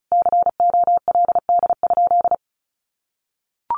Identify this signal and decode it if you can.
Morse code, heard as a keyed tone, about 34 wpm, 700 Hz. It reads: COLD?
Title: Sentences from Top 1500 Words 35wpm